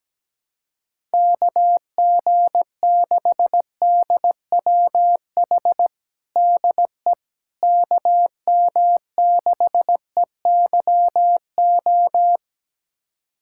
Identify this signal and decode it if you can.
Morse code, heard as a keyed tone, about 17 wpm, 700 Hz. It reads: KG6DWH DE KM6EYO
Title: USA Callsign DE Callsign - 6 Characters - 17wpm